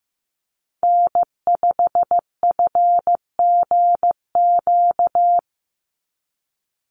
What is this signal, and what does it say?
Morse code, heard as a keyed tone, about 15 wpm, 700 Hz. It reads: N5FGQ
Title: USA Call Signs 15wpm